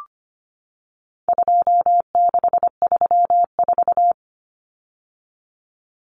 Morse code, heard as a keyed tone, about 25 wpm, 700 Hz. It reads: 2634
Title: Numbers - 4-Digit 25wpm